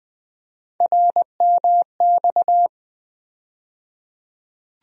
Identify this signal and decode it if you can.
Morse code, heard as a keyed tone, about 20 wpm, 700 Hz. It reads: RMX
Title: USA Call Sign Postfixes - 3 Letters 20wpm